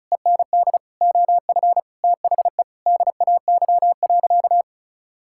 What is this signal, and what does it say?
Morse code, heard as a keyed tone, about 36 wpm, 700 Hz. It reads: END OF THE DAY.
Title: Sentences from Top 1000 Words - Review Part 6 - 35wpm